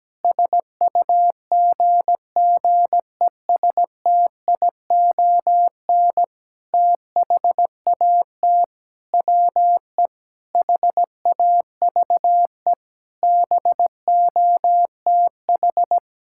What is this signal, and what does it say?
Morse code, heard as a keyed tone, about 17 wpm, 700 Hz. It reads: SUGGESTION THAT WE HAVE BOTH
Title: Sentences from Top 1600 Words 17wpm